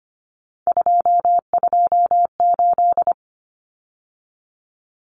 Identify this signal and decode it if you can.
Morse code, heard as a keyed tone, about 25 wpm, 700 Hz. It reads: 228
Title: Numbers - 3-Digit 25wpm